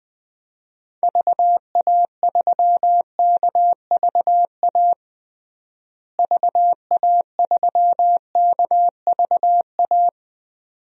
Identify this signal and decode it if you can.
Morse code, heard as a keyed tone, about 20 wpm, 700 Hz. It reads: VA3KVA VA3KVA
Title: DXCC-100 Call Signs - 6 Characters - Repeated 2x - 20wpm